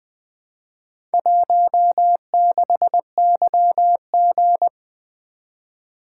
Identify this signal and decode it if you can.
Morse code, heard as a keyed tone, about 20 wpm, 700 Hz. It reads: 16YG
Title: Four Letter-Number Combinations - 20wpm